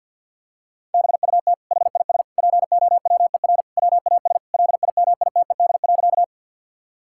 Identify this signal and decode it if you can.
Morse code, heard as a keyed tone, about 50 wpm, 700 Hz. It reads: BUT HIS POWER WAS LIMITED.
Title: Sentences from Top 1000 Words - Review Part 6 - 50wpm